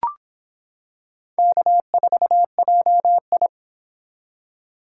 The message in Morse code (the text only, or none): K4JI